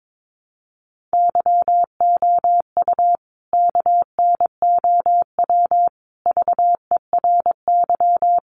Morse code, heard as a keyed tone, about 22 wpm, 700 Hz. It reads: YOU KNOW VERY
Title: A Sound Of Thunder 22wpm